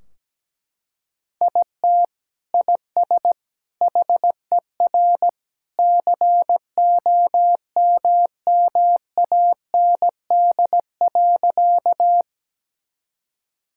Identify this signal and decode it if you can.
Morse code, heard as a keyed tone, about 17 wpm, 700 Hz. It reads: IT IS HER COMMAND.
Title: Sentences from Top 1600 Words 17wpm